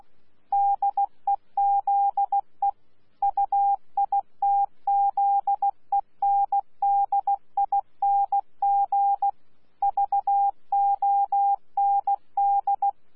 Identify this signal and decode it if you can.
Morse code, heard as a keyed tone, about 16 wpm, 800 Hz. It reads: DEZE UITZENDING VOND
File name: Z PI4VRZ / PI4VRZ 14-jun-2025